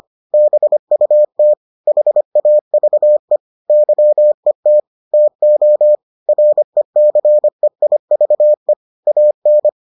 Morse code, heard as a keyed tone, about 25 wpm, 600 Hz. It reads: BUT HAVE YET TO RECEIVE AN